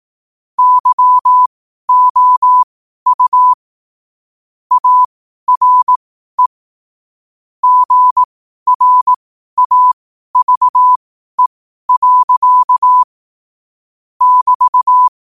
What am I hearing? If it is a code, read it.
Morse code, heard as a keyed tone, about 18 wpm, spaced as for 12 wpm, 1000 Hz. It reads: YOU ARE GRAVE. <BT>
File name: qotdfiles / 12wpm